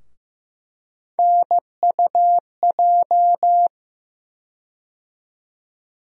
Morse code, heard as a keyed tone, about 15 wpm, 700 Hz. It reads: NUJ
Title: USA Call Sign Postfixes - 3 Letters 15wpm